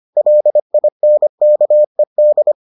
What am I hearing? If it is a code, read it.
Morse code, heard as a keyed tone, about 25 wpm, 600 Hz. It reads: LINKED